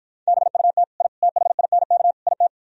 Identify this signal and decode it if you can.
Morse code, heard as a keyed tone, about 52 wpm, 700 Hz. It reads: BUT I THINK IT